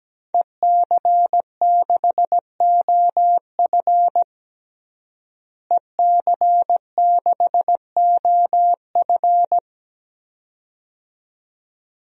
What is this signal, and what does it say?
Morse code, heard as a keyed tone, about 17 wpm, 700 Hz. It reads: EC6OF EC6OF
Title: DXCC-100 Call Signs - 5 Characters - Repeated 2x - 17wpm